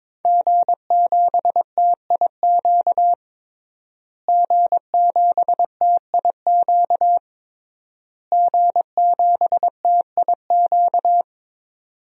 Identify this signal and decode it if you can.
Morse code, heard as a keyed tone, about 23 wpm, 700 Hz. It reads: G7TIQ G7TIQ G7TIQ
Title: DXCC-100 Call Signs - 5 Characters - Repeated 3x - 22wpm